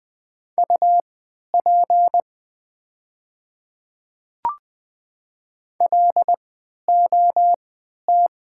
Morse code, heard as a keyed tone, about 20 wpm, spaced as for 11 wpm, 700 Hz. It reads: UP LOT
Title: 3x Character Spacing - Top 500 Words - 20wpm